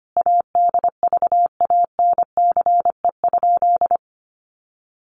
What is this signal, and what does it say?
Morse code, heard as a keyed tone, about 25 wpm, 700 Hz. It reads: ADVANCE?
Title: Sentences from Top 1400 Words 25wpm